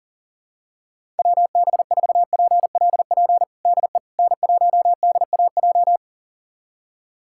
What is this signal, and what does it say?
Morse code, heard as a keyed tone, about 37 wpm, 700 Hz. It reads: WB4PLP DE N1DAJ
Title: USA Callsign DE Callsign 40wpm